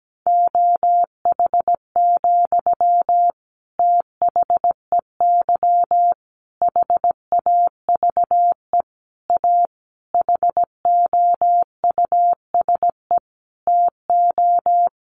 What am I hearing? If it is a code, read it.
Morse code, heard as a keyed tone, about 17 wpm, 700 Hz. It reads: OH, THEY HAVE A HOUSE TO
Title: Sentences from Top 800 Words 17wpm